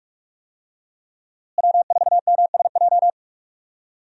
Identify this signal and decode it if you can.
Morse code, heard as a keyed tone, about 43 wpm, 700 Hz. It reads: W4MSJ
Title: USA Call Signs - 5 Characters - Repeated 2x - 45wpm